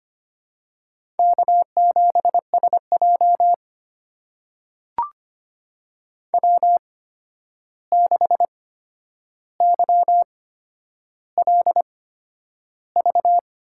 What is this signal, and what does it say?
Morse code, heard as a keyed tone, about 25 wpm, spaced as for 10 wpm, 700 Hz. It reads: K7SJ W6YLV
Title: USA Call Signs - 8x Character Spacing - 25wpm